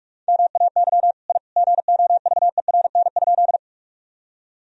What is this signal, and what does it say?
Morse code, heard as a keyed tone, about 41 wpm, 700 Hz. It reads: MAY I GOVERN?
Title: Sentences from Top 1000 Words - Review Part 2 - 45wpm